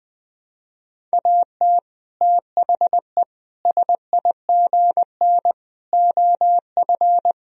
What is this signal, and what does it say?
Morse code, heard as a keyed tone, about 21 wpm, 700 Hz. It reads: AT THE SIGN OF
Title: Sentences from Top 1000 Words - Review Part 1 - 20wpm